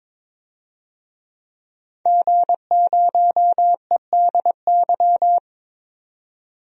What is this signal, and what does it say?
Morse code, heard as a keyed tone, about 22 wpm, 700 Hz. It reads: G0EDY
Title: DXCC-100 Call Signs - 5 Characters - 22wpm